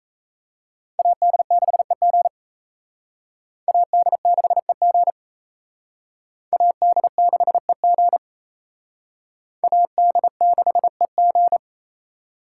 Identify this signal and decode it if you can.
Morse code, heard as a keyed tone, about 35 wpm, 700 Hz. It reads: AD6EG AD6EG AD6EG AD6EG